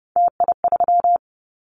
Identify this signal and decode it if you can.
Morse code, heard as a keyed tone, about 30 wpm, 700 Hz. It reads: TI3
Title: Callsign Prefixes for Top 100 DXCC - 30wpm